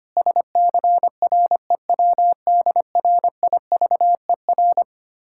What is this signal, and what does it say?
Morse code, heard as a keyed tone, about 25 wpm, 690 Hz. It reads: SCREWDRIVER
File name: practice / 25wpm / Difficult Words with Hints - 25wpm